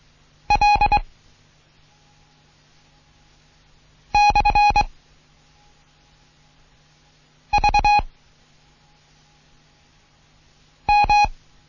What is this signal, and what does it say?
Morse code, heard as a keyed tone, about 23 wpm, 830 Hz. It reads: L / V M
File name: Morsen / Kurs A 16